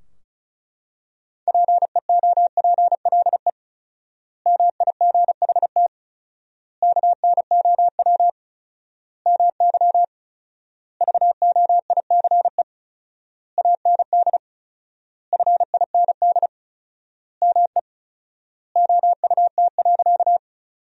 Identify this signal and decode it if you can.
Morse code, heard as a keyed tone, about 36 wpm, 700 Hz. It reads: PEOPLE MIGHT KNOW MY VOICE AND FIND ME OUT.
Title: 4x Word-Spacing - Sentences from Top 500 Words - 35wpm